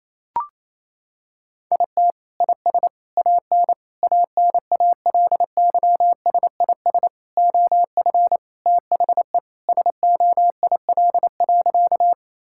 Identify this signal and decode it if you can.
Morse code, heard as a keyed tone, about 28 wpm, 700 Hz. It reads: IT IS AN ANALYSIS OF THE SOIL.
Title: Sentences from Top 1000 Words - Review Part 5 - 28wpm